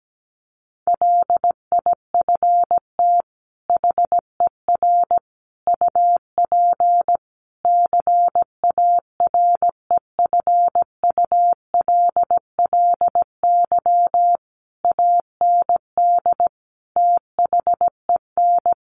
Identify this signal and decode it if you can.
Morse code, heard as a keyed tone, about 17 wpm, 700 Hz. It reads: LIFT HER UP CAREFULLY AND THEN
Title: Sentences from Top 1300 Words 17wpm